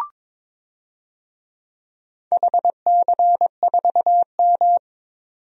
Morse code, heard as a keyed tone, about 21 wpm, 700 Hz. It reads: HC4M
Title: DXCC-100 Call Signs - 4 Characters - 22wpm